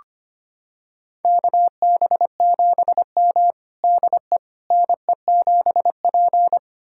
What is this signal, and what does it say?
Morse code, heard as a keyed tone, about 25 wpm, 700 Hz. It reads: KB7M DE NE7P
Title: USA Callsign DE Callsign - 4 Characters - 25wpm